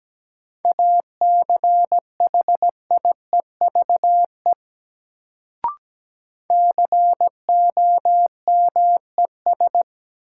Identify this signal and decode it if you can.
Morse code, heard as a keyed tone, about 17 wpm, 700 Hz. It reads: ACHIEVE COMES
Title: Sentences from Top 1000 Words - Review Part 5 - New Word Drill - 17wpm